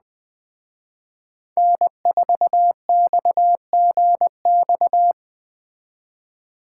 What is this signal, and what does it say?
Morse code, heard as a keyed tone, about 20 wpm, 700 Hz. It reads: N4XGX
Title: Speed Racing - USA 1x3 Call Signs 17wpm